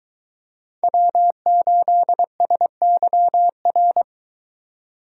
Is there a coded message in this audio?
Yes: Morse code from 0.8 to 4.0 s, heard as a keyed tone, about 22 wpm, 700 Hz. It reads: W8SYR